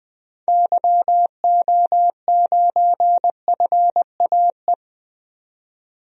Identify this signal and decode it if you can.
Morse code, heard as a keyed tone, about 20 wpm, 700 Hz. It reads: YO9FAE
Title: DXCC-100 Call Signs - 6 Characters - 20wpm